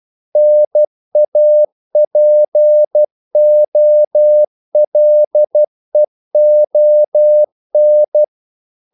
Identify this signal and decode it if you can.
Morse code, heard as a keyed tone, about 12 wpm, 600 Hz. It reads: NAPOLEON
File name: CW / Practice / EnglishQuotations0112